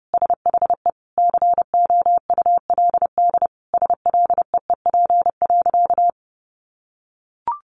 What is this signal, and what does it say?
Morse code, heard as a keyed tone, about 30 wpm, 700 Hz. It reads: SHE COULD SLEEP.
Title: Sentences from Top 500 Words 30wpm